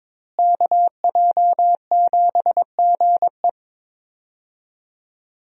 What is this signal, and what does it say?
Morse code, heard as a keyed tone, about 22 wpm, 700 Hz. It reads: KJ7GE
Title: USA Call Signs - 5 Characters - Repeated 3x - 22wpm